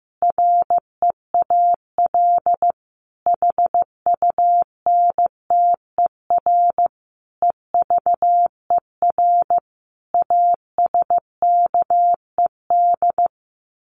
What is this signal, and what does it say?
Morse code, heard as a keyed tone, about 15 wpm, 700 Hz. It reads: REAL HUNTER EVER ASKED
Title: A Sound Of Thunder 15wpm